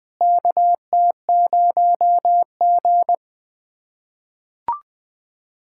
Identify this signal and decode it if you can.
Morse code, heard as a keyed tone, about 21 wpm, 700 Hz. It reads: KT0G
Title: K1USN SST QSO Practice - 20wpm